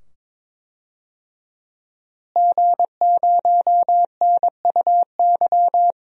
Morse code, heard as a keyed tone, about 22 wpm, 700 Hz. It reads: G0NUY
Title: DXCC-100 Call Signs - 5 Characters - Repeated 2x - 22wpm